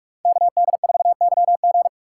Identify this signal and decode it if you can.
Morse code, heard as a keyed tone, about 49 wpm, 700 Hz. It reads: KD4YG